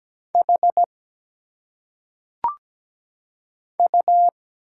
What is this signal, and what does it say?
Morse code, heard as a keyed tone, about 17 wpm, 700 Hz. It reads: H U
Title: Single Letters - Phonetic Pronunciation 17wpm